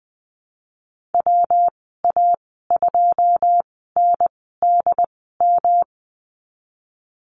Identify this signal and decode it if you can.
Morse code, heard as a keyed tone, about 20 wpm, spaced as for 15 wpm, 700 Hz. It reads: WA2NDM
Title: 179 Lesson 29g - Slash - Call Signs 20wpm